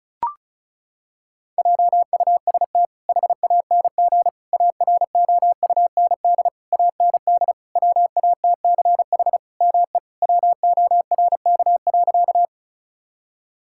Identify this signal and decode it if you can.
Morse code, heard as a keyed tone, about 36 wpm, 700 Hz. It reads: JUST HANG AROUND AND WATCH ME WORK.
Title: Sentences from Top 1600 Words 35wpm